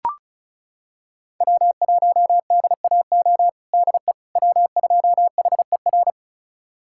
Morse code, heard as a keyed tone, about 34 wpm, 700 Hz. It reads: W1DAO DE W2HER
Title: USA Callsign DE Callsign - 5 Characters - 35wpm